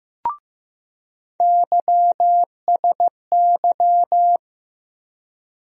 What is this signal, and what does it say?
Morse code, heard as a keyed tone, about 15 wpm, 700 Hz. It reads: YSY
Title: USA Call Sign Postfixes - 3 Letters 15wpm